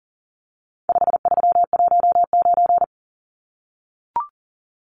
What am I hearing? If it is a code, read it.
Morse code, heard as a keyed tone, about 40 wpm, 700 Hz. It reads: 5319
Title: Numbers - 4-Digit 40wpm